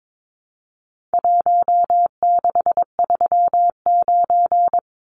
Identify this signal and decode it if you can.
Morse code, heard as a keyed tone, about 22 wpm, 700 Hz. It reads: 1639
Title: Numbers - Contest Style - 4-Digit 22wpm